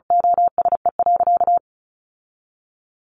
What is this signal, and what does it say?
Morse code, heard as a keyed tone, about 34 wpm, 700 Hz. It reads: OSE.
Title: Sentences from Top 1200 Words 35wpm